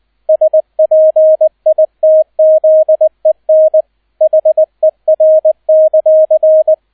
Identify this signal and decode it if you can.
Morse code, heard as a keyed tone, about 20 wpm, 630 Hz, with radio noise. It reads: SPITZEN HER;
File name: 002 Zweiter Streich 20wpm